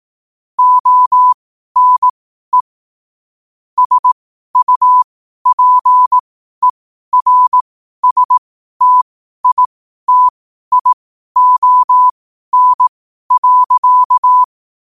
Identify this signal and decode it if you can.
Morse code, heard as a keyed tone, about 18 wpm, spaced as for 12 wpm, 1000 Hz. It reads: ONE SUPERSTITION.
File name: qotdfiles / 12wpm